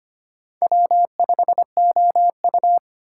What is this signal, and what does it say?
Morse code, heard as a keyed tone, about 25 wpm, 700 Hz. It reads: W5OU
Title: USA Call Signs - 2x Character Spacing - 25wpm